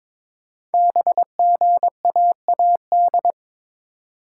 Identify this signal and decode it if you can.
Morse code, heard as a keyed tone, about 23 wpm, 700 Hz. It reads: BGAAD
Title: Code Groups - 5 Letters - 22wpm